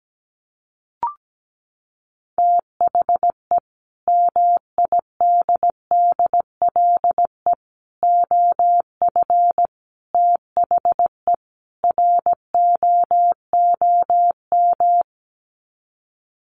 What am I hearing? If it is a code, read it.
Morse code, heard as a keyed tone, about 17 wpm, 700 Hz. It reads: THE MIDDLE OF THE ROOM
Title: Sets of 5 Words Encore2 17wpm